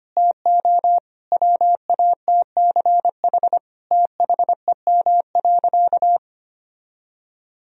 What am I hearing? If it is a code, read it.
Morse code, heard as a keyed tone, about 25 wpm, 700 Hz. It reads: TO WATCH THEM.
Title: Sentences from Top 1000 Words - Review Part 1 - 25wpm